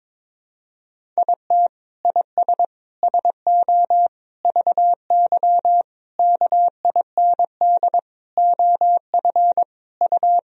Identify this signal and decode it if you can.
Morse code, heard as a keyed tone, about 22 wpm, 700 Hz. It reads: IT IS SO VY KIND OF U
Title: Sentences from Top 300 Words and Abbreviations 22wpm